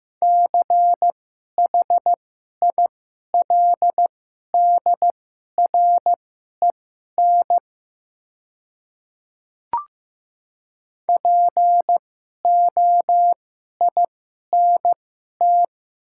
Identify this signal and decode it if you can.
Morse code, heard as a keyed tone, about 15 wpm, spaced as for 11 wpm, 700 Hz. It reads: CHILDREN POINT
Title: 2x Character Spacing - Top 500 Words - 15wpm